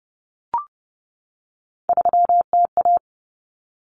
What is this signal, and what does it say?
Morse code, heard as a keyed tone, about 30 wpm, 700 Hz. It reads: 3TA
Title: Common Cut Numbers - 3-Digit 30wpm